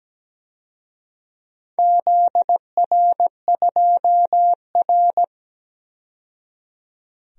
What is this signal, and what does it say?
Morse code, heard as a keyed tone, about 17 wpm, 700 Hz. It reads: ZR2R
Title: DXCC-100 Call Signs - 4 Characters - 17wpm